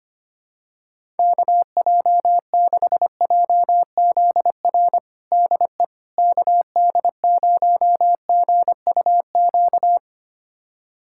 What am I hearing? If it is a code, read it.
Morse code, heard as a keyed tone, about 25 wpm, 700 Hz. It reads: KJ6JZR DE KD0GUQ
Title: USA Callsign DE Callsign 25wpm